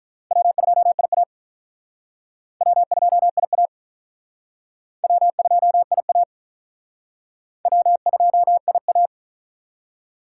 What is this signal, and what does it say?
Morse code, heard as a keyed tone, about 42 wpm, 700 Hz. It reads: W2IA W2IA W2IA W2IA
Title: Speed Racing - USA 1x2 Call Signs 35wpm